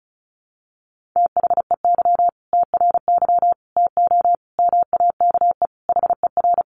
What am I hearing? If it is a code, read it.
Morse code, heard as a keyed tone, about 34 wpm, 700 Hz. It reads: THEY TRY TO MAKE HER
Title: Easy Phrases 35wpm